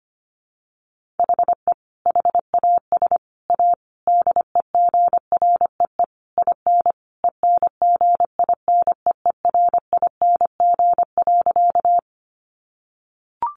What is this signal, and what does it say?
Morse code, heard as a keyed tone, about 25 wpm, 700 Hz. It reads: HE HAS A DEGREE IN ENGINEERING.
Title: Sentences from Top 1500 Words 25wpm